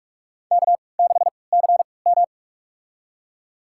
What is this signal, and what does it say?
Morse code, heard as a keyed tone, about 45 wpm, spaced as for 25 wpm, 700 Hz. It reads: K6CM
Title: USA Call Signs - 4 Characters - 3x Character Spacing - 45wpm